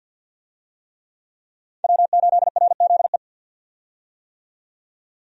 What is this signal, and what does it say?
Morse code, heard as a keyed tone, about 51 wpm, 700 Hz. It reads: W8RZE
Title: USA Call Signs - 5 Characters - 50wpm